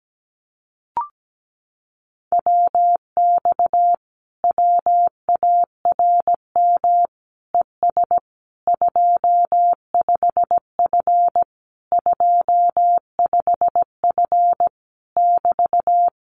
Morse code, heard as a keyed tone, about 17 wpm, 700 Hz. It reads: WX WARM ES 25F 25F <BT>
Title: Common QSO Phrases Encore 17wpm